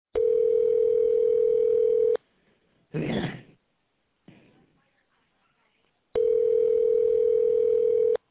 Ringback tone (North America), 440 + 480 Hz.